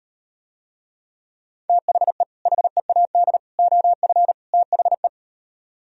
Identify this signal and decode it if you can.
Morse code, heard as a keyed tone, about 38 wpm, 700 Hz. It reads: THE HEAD OF THE